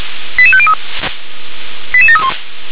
Five-tone selective call (ZVEI), heard as quick runs of five tones, tones of about 70 ms. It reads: 88503 88731